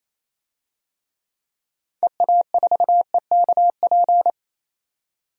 Morse code, heard as a keyed tone, about 28 wpm, 700 Hz. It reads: EA4EKP
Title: DXCC-100 Call Signs - 6 Characters - 28wpm